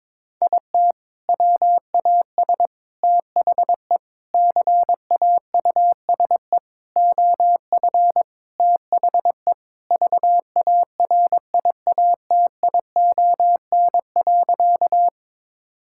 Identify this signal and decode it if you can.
Morse code, heard as a keyed tone, about 21 wpm, 700 Hz. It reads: IT WAS THE CAUSE OF THE VARIATION.
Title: Sentences from Top 1600 Words 22wpm